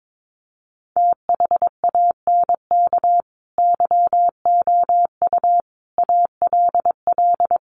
Morse code, heard as a keyed tone, about 22 wpm, 700 Hz. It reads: THANK YOU ALL
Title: Sets of 5 Words Encore 22wpm